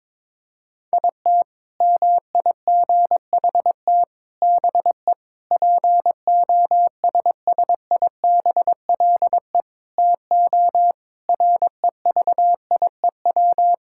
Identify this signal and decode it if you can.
Morse code, heard as a keyed tone, about 22 wpm, 700 Hz. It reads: IT MIGHT BE POSSIBLE TO REVIEW
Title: Sentences from Top 1000 Words - Review Part 6 - 22wpm